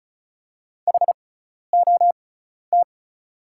Morse code, heard as a keyed tone, about 36 wpm, spaced as for 12 wpm, 700 Hz. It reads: HOT
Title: 6x Character Spacing - Top 200 Words - 35wpm